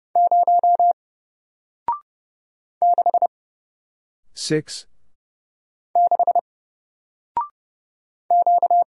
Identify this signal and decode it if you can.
Morse code, heard as a keyed tone, about 29 wpm, 700 Hz. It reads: 0 6 6 Q
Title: Single Letter-Number Alt 30wpm